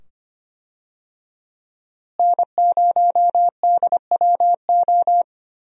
Morse code, heard as a keyed tone, about 25 wpm, 700 Hz. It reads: N0DWO
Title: USA Call Signs - 5 Characters - Repeated 2x - 25wpm